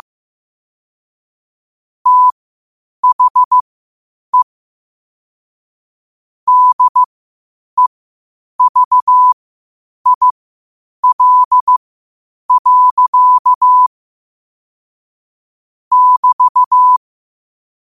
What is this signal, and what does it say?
Morse code, heard as a keyed tone, about 15 wpm, spaced as for 8 wpm, 1000 Hz. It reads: THE DEVIL. <BT>
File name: 8wpm